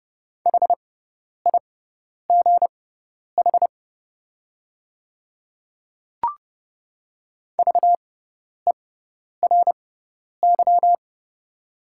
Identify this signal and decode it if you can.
Morse code, heard as a keyed tone, about 30 wpm, spaced as for 10 wpm, 700 Hz. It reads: HIGH VERY
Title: 6x Character Spacing - Top 300 Words - 30wpm